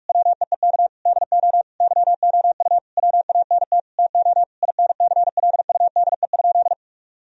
Morse code, heard as a keyed tone, about 46 wpm, 700 Hz. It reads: WEEK DO YOU WANT TO INCLUDE?
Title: Sentences from Top 1000 Words - Review Part 3 - 45wpm